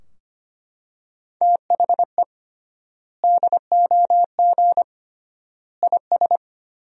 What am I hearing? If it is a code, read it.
Morse code, heard as a keyed tone, about 25 wpm, 700 Hz. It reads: THE DOG IS
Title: 3x Word-Spacing - Sentences from Top 400 Words - 25wpm